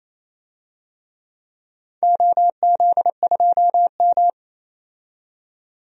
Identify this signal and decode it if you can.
Morse code, heard as a keyed tone, about 28 wpm, 700 Hz. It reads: OZ2M
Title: DXCC-100 Call Signs - 4 Characters - 28wpm